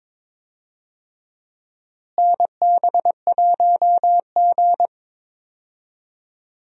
Morse code, heard as a keyed tone, about 22 wpm, 700 Hz. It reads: NB1G
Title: USA Call Signs - 4 Characters - Repeated 3x - 22wpm